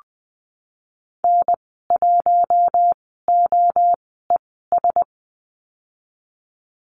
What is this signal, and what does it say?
Morse code, heard as a keyed tone, about 20 wpm, spaced as for 14 wpm, 700 Hz. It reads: N1OES
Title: 048 Lesson 9g - Number 4 - Call Signs 20wpm